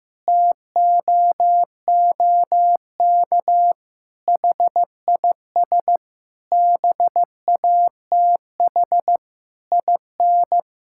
Morse code, heard as a keyed tone, about 15 wpm, 700 Hz. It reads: TOOK HIS BATH IN
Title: Sentences from Top 1600 Words 15wpm